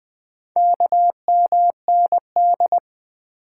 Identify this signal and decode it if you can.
Morse code, heard as a keyed tone, about 20 wpm, 700 Hz. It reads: KMND
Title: Four Letter-Number Combinations - 20wpm